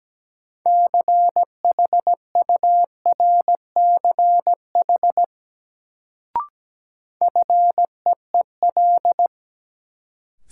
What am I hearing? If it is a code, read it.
Morse code, heard as a keyed tone, about 17 wpm, 700 Hz. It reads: CHURCH FEEL
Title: Sets of 5 Words - New Word Drill - 17wpm